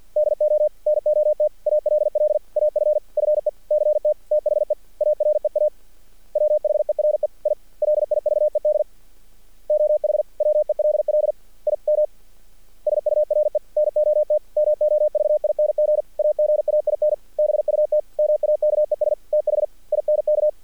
Morse code, heard as a keyed tone, about 48 wpm, 600 Hz. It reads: DO NOT ALL AFFECT THE AREA WHERE I LIVED OH WELL IM SURE NOT MOVING AGAIN BUT MAKES THINK
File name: archived / CRSnet